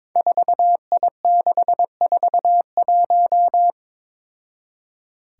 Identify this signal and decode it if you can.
Morse code, heard as a keyed tone, about 22 wpm, 700 Hz. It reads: 4I641
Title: Five Challenging Letter-Number Combinations 22wpm